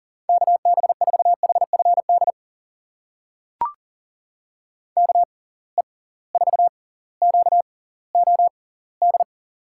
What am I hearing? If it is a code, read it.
Morse code, heard as a keyed tone, about 40 wpm, spaced as for 21 wpm, 700 Hz. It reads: KB4HFD KE4QOD
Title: USA Call Signs - 6 Characters - 6x Character Spacing - 40wpm